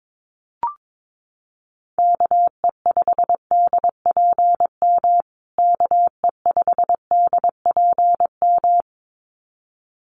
Morse code, heard as a keyed tone, about 22 wpm, 700 Hz. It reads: KE5DPM KE5DPM
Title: USA Call Signs Repeated 2x 22wpm